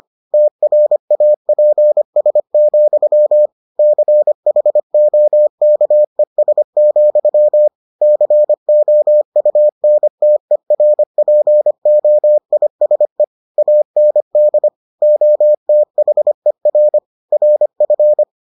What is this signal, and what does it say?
Morse code, heard as a keyed tone, about 25 wpm, 600 Hz. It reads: TRAPS, CHOKES, COUNTERPOISE AND OTHER RF